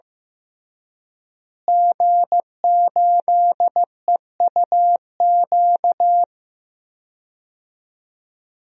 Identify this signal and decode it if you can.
Morse code, heard as a keyed tone, about 16 wpm, 700 Hz. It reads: G8EUQ
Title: DXCC-100 Call Signs - 5 Characters - Repeated 3x - 15wpm